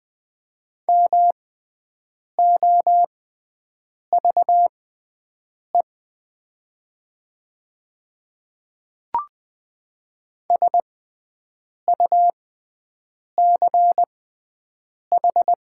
Morse code, heard as a keyed tone, about 20 wpm, spaced as for 7 wpm, 700 Hz. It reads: MOVE SUCH
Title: 6x Character Spacing - Top 300 Words - 20wpm